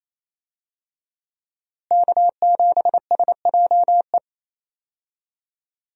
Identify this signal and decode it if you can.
Morse code, heard as a keyed tone, about 28 wpm, 700 Hz. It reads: K7SJE
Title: USA Call Signs - 5 Characters - Repeated 2x - 28wpm